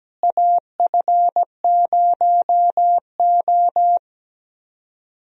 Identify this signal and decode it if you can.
Morse code, heard as a keyed tone, about 17 wpm, 700 Hz. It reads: AF0O
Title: USA Call Signs - 4 Characters - Repeated 3x - 17wpm